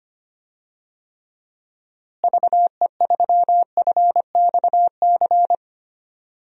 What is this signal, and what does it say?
Morse code, heard as a keyed tone, about 25 wpm, 700 Hz. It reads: VE3FXC
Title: DXCC-100 Call Signs - 6 Characters - Repeated 3x - 25wpm